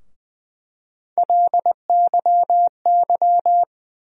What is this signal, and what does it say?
Morse code, heard as a keyed tone, about 20 wpm, 700 Hz. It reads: LYY
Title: USA Call Sign Postfixes - 3 Letters 20wpm